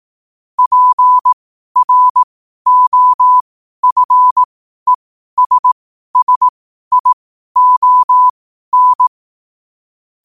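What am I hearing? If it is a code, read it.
Morse code, heard as a keyed tone, about 18 wpm, spaced as for 12 wpm, 1000 Hz. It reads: PROFESSION
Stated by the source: Morse code quote practice at 12 words per minute.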